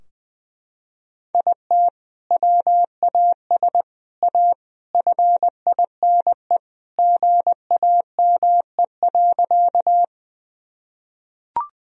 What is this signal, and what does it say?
Morse code, heard as a keyed tone, about 20 wpm, 700 Hz. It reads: IT WAS A FINE GAME.
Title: Sentences from Top 1000 Words - Review Part 3 - 20wpm